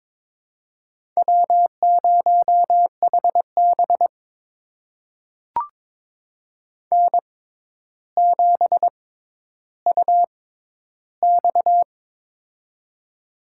Morse code, heard as a keyed tone, about 22 wpm, spaced as for 11 wpm, 700 Hz. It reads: W0HB N7UX
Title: USA Call Signs - 4 Characters - 6x Character Spacing - 22wpm